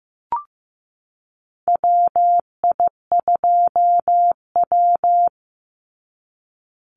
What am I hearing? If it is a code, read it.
Morse code, heard as a keyed tone, about 15 wpm, 700 Hz. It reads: WI2W
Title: NA SOTA Chaser Callsigns 15wpm